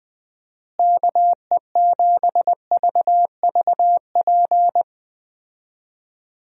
Morse code, heard as a keyed tone, about 20 wpm, 700 Hz. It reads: KE7VVP
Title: USA Call Signs - 6 Characters - Repeated 3x - 20wpm